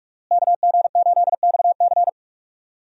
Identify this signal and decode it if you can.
Morse code, heard as a keyed tone, about 46 wpm, 700 Hz. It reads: KG8XC
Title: USA Call Signs - 5 Characters - Repeated 3x - 45wpm